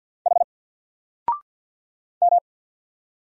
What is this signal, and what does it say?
Morse code, heard as a keyed tone, about 48 wpm, 700 Hz. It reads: H M
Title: Single Letter-Number 50wpm